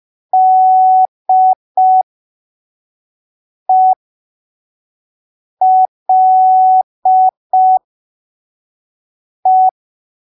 Morse code, heard as a keyed tone, about 5 wpm, spaced as for 3 wpm, 750 Hz. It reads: DELE
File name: arquivos / texto / texto 17 05ppm